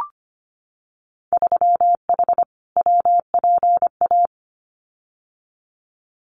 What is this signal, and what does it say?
Morse code, heard as a keyed tone, about 25 wpm, 700 Hz. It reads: 3H WPA
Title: Winter Field Day Exchanges Encore 25wpm